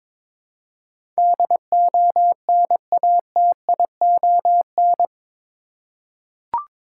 Morse code, heard as a keyed tone, about 22 wpm, 700 Hz. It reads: DONATION
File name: practice / 22wpm / Eight Letter Words - 22wpm